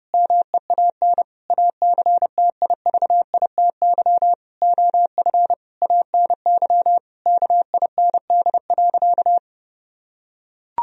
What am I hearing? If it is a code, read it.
Morse code, heard as a keyed tone, about 30 wpm, 700 Hz. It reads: MEAN ACTIVITY OF ANY KIND.
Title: Sentences from Top 1000 Words - Review Part 4 - 30wpm